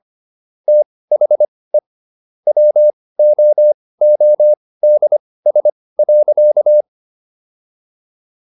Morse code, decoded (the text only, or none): THE WOODS.